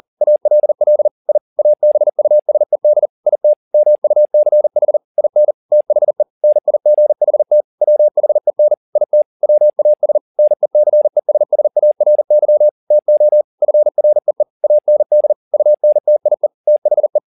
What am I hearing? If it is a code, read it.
Morse code, heard as a keyed tone, about 40 wpm, 600 Hz. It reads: ALL I ABUSED IT MUCH IN THE NIGHT WHEN IT WAS NECESSARY TO FREE AND UNTIE THE